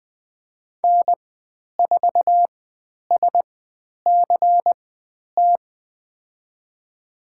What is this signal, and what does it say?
Morse code, heard as a keyed tone, about 20 wpm, spaced as for 10 wpm, 700 Hz. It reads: N4SCT
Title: 095 Lesson 16g - Letter c - Call Signs 20wpm